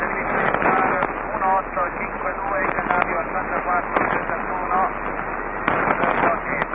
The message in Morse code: none